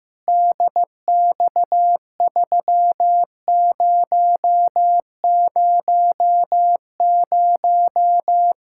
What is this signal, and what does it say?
Morse code, heard as a keyed tone, about 15 wpm, 700 Hz. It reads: DX3000
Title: K1USN SST QSO Practice 15wpm